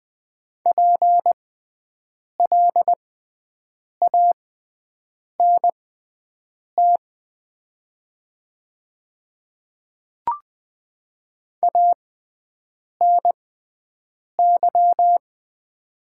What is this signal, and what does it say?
Morse code, heard as a keyed tone, about 20 wpm, spaced as for 7 wpm, 700 Hz. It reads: PLANT ANY
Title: 6x Character Spacing - Top 300 Words - 20wpm